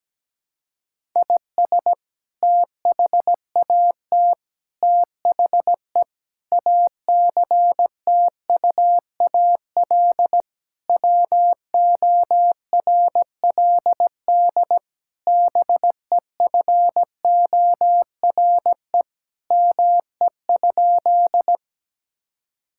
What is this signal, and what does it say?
Morse code, heard as a keyed tone, about 17 wpm, 700 Hz. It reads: IS THAT THE ACTUAL WORLD BEFORE ME?
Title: Sentences from Top 1000 Words - Review Part 5 - 17wpm